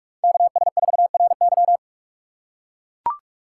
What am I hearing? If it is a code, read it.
Morse code, heard as a keyed tone, about 43 wpm, 700 Hz. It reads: KS4RY